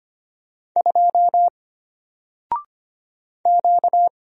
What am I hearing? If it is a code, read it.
Morse code, heard as a keyed tone, about 25 wpm, 700 Hz. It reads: 2 Q
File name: Single-Letter-Number-25wpm